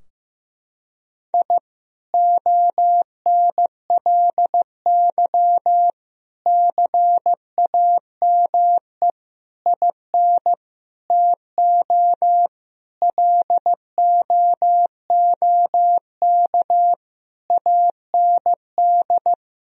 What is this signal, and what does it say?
Morse code, heard as a keyed tone, about 15 wpm, 700 Hz. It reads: I ONLY CAME IN TO LOOK AND
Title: Sentences from Top 1000 Words - Review Part 2 - 15wpm